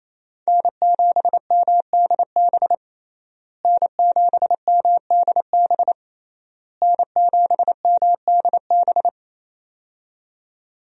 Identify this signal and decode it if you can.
Morse code, heard as a keyed tone, about 28 wpm, 700 Hz. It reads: N7MDB N7MDB N7MDB
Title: USA Call Signs - 5 Characters - Repeated 3x - 28wpm